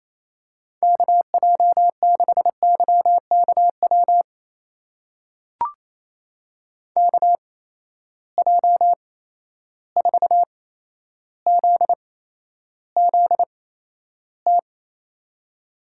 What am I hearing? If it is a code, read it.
Morse code, heard as a keyed tone, about 28 wpm, spaced as for 12 wpm, 700 Hz. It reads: KJ6YKW KJ4ZZT
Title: USA Call Signs - 6 Characters - 8x Character Spacing - 28wpm